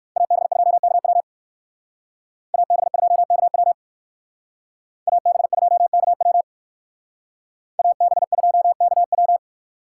Morse code, heard as a keyed tone, about 58 wpm, 700 Hz. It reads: AB2KW AB2KW AB2KW AB2KW